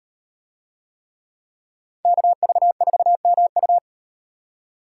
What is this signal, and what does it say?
Morse code, heard as a keyed tone, about 37 wpm, 700 Hz. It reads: KV4MU